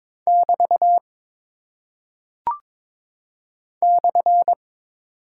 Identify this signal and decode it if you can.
Morse code, heard as a keyed tone, about 22 wpm, 700 Hz. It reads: <BT> /
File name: practice / 22wpm / Single Punctuation-Prosign - 22wpm